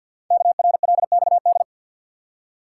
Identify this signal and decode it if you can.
Morse code, heard as a keyed tone, about 50 wpm, 700 Hz. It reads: KRLXD
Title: Code Groups - 5 Letters - 50wpm